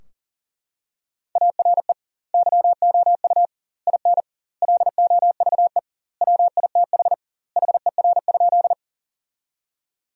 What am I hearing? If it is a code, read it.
Morse code, heard as a keyed tone, about 40 wpm, 700 Hz. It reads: ARE YOU IN LOVE WITH HER?
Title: 2x Word-Spacing - Sentences from Top 400 Words - 40wpm